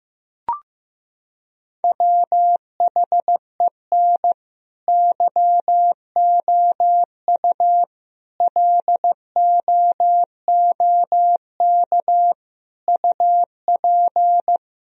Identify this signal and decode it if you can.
Morse code, heard as a keyed tone, about 15 wpm, 700 Hz. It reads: WHEN YOU LOOK UP
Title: Sentences from Top 1000 Words - Review Part 1 - 15wpm